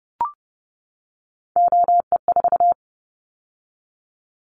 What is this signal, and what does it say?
Morse code, heard as a keyed tone, about 30 wpm, 700 Hz. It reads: OE4
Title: Callsign Prefixes for Top 100 DXCC - 30wpm